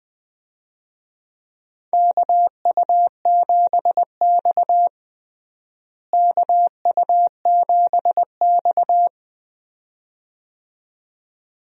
Morse code, heard as a keyed tone, about 20 wpm, 700 Hz. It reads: KU7X KU7X
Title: USA Call Signs - 4 Characters - Repeated 2x - 20wpm